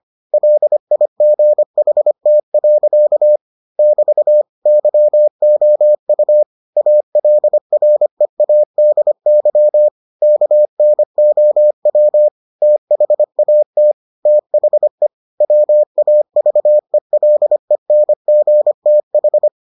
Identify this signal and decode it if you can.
Morse code, heard as a keyed tone, about 25 wpm, 600 Hz. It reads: LIGHT. <BT> YOU ALREADY KNOW THAT THE WAVELENGTH